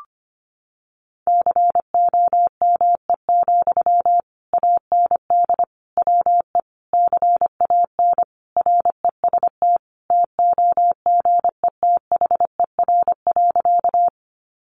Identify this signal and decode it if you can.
Morse code, heard as a keyed tone, about 25 wpm, 700 Hz. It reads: COME, AND WE CAN REST TOGETHER.
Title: Sentences from Top 500 Words 25wpm